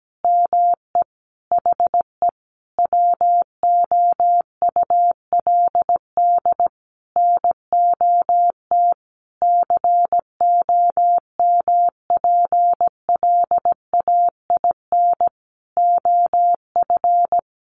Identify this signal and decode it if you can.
Morse code, heard as a keyed tone, about 17 wpm, 700 Hz. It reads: ME HE WOULD NOT COMPLAIN OF
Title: Sentences from Top 1300 Words 17wpm